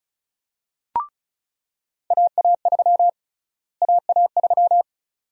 Morse code, heard as a keyed tone, about 36 wpm, 700 Hz. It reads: AA3 AA3
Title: USA Call Sign Prefixes - Repeated 2x - 35wpm